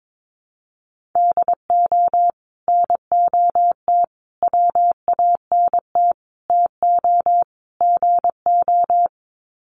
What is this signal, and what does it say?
Morse code, heard as a keyed tone, about 22 wpm, 700 Hz. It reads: DO NOT WANT TO GO
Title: Sets of 5 Words 22wpm